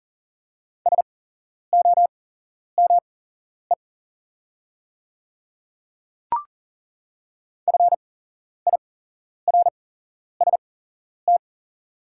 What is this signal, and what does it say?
Morse code, heard as a keyed tone, about 40 wpm, spaced as for 11 wpm, 690 Hz. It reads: SOME FIRST
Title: 8x Character Spacing - Top 100 Words - 40wpm